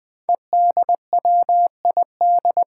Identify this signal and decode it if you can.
Morse code, heard as a keyed tone, about 20 wpm, 700 Hz. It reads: EDWID
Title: Five Challenging Letter-Number Combinations 20wpm